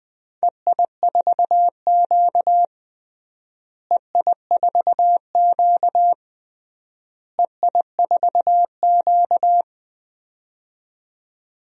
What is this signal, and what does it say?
Morse code, heard as a keyed tone, about 21 wpm, 700 Hz. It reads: EI4Q EI4Q EI4Q
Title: DXCC-100 Call Signs - 4 Characters - Repeated 3x - 20wpm